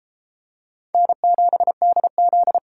Morse code, heard as a keyed tone, about 31 wpm, 700 Hz. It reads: N7DZ